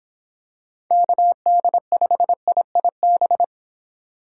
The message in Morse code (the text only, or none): KD5IIB